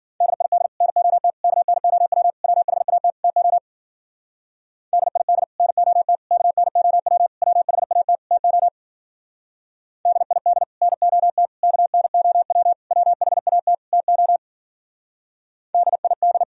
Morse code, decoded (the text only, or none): DID NOT KNOW WHAT TO DID NOT KNOW WHAT TO DID NOT KNOW WHAT TO DID